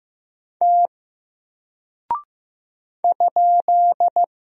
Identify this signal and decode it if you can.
Morse code, heard as a keyed tone, about 15 wpm, 700 Hz. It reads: T ?
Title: Single Character (Letter:Number:Punctuation:Prosign) Alt 15wpm